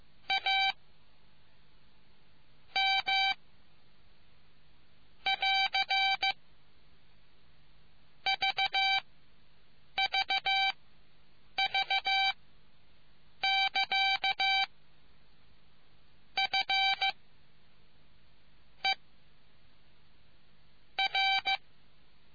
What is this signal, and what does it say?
Morse code, heard as a keyed tone, about 15 wpm, spaced as for 5 wpm, 2350 Hz. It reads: AM <AR> VVV <KA> FER